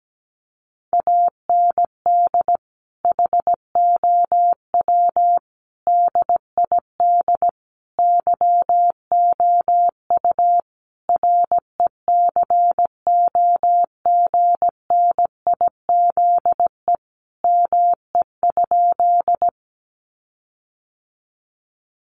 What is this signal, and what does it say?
Morse code, heard as a keyed tone, about 17 wpm, 700 Hz. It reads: AND HOW DID YOU RECOGNIZE ME?
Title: Sentences from Top 1400 Words 17wpm